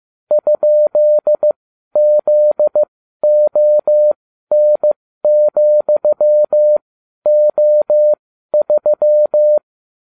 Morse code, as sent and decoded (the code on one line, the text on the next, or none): ..--.. --.. --- -. --..-- --- ...--
?ZON,O3